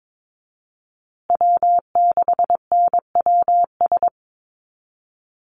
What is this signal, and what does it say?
Morse code, heard as a keyed tone, about 22 wpm, 700 Hz. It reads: W6NWS
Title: Call Signs Repeated 2x - 22wpm